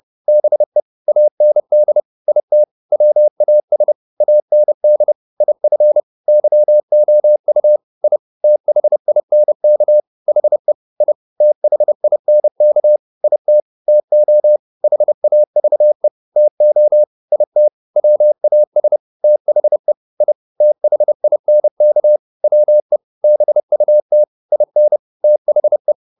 Morse code, heard as a keyed tone, about 30 wpm, 600 Hz. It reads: BE AND IT WAS AND IF YOU I THINK HE I THINK IT TO HAVE TO IT WAS THE I THINK WE BUT IN THE